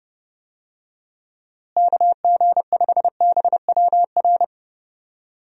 Morse code, heard as a keyed tone, about 30 wpm, 700 Hz. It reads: KG5BWR
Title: USA Call Signs - 6 Characters - Repeated 3x - 30wpm